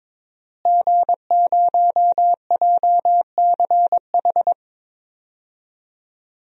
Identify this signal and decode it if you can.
Morse code, heard as a keyed tone, about 22 wpm, 700 Hz. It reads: G0JCH
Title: DXCC-100 Call Signs - 5 Characters - Repeated 2x - 22wpm